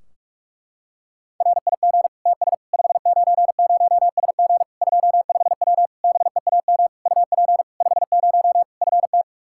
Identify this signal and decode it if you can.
Morse code, heard as a keyed tone, about 46 wpm, 700 Hz. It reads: RIG TS 590SG 25W BEAM UP 50 FT
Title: Practice QSO - Exchange 2 - Info - 45wpm